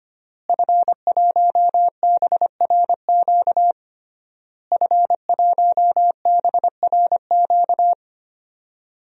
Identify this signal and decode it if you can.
Morse code, heard as a keyed tone, about 25 wpm, 700 Hz. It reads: F1BRQ F1BRQ
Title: DXCC-100 Call Signs - 5 Characters - Repeated 3x - 25wpm